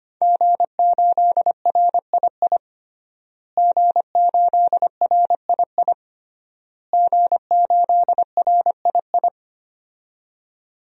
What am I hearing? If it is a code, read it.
Morse code, heard as a keyed tone, about 25 wpm, 700 Hz. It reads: G8RII G8RII G8RII
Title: DXCC-100 Call Signs - 5 Characters - Repeated 3x - 25wpm